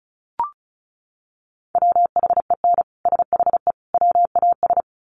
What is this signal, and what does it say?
Morse code, heard as a keyed tone, about 34 wpm, 700 Hz. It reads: WHEN SHE WAS
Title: Sets of 3 Words 35wpm